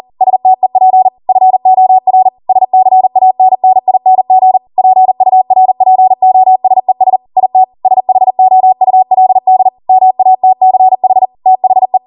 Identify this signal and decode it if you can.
Morse code, heard as a keyed tone, about 40 wpm, 750 Hz. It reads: STEP FOR SCANNING PURPOSES IT SHOULD MATCH THE
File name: Morse / 40WPM